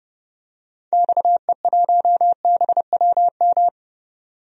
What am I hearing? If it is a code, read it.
Morse code, heard as a keyed tone, about 30 wpm, 700 Hz. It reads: XE1BWM
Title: DXCC-100 Call Signs - 6 Characters - 30wpm